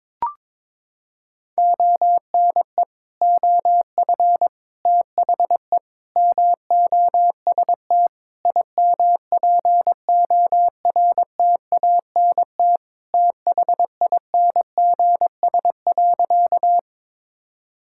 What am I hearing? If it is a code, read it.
Morse code, heard as a keyed tone, about 22 wpm, 700 Hz. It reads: ONE OF THE MOST IMPORTANT THINGS.
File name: practice / 22wpm / Sentences with 4-Word Phrases - 22wpm